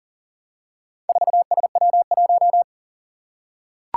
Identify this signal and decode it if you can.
Morse code, heard as a keyed tone, about 38 wpm, 700 Hz. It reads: 4SW1
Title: Four Challenging Letter-Number Combinations 40wpm